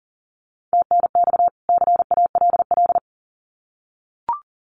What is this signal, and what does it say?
Morse code, heard as a keyed tone, about 40 wpm, 700 Hz. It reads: TNX CALL
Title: Common QSO Phrases 40wpm